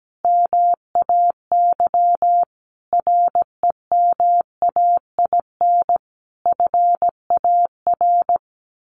Morse code, heard as a keyed tone, about 17 wpm, 700 Hz. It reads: MAY REMAIN FAR
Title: Sentences from Top 1400 Words 17wpm